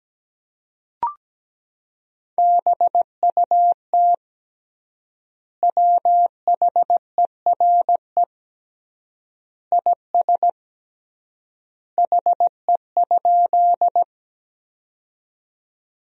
Morse code, heard as a keyed tone, about 17 wpm, 700 Hz. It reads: BUT WHERE IS HE?
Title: 3x Word-Spacing - Sentences from Top 200 Words - 17wpm